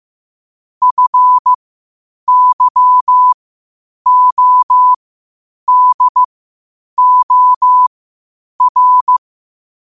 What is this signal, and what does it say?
Morse code, heard as a keyed tone, about 15 wpm, spaced as for 8 wpm, 1000 Hz. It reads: FYODOR
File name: qotdfiles / 8wpm